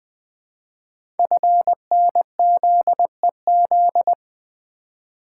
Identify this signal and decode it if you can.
Morse code, heard as a keyed tone, about 20 wpm, 700 Hz. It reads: FNZEZ
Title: Code Groups - 5 Letters - 20wpm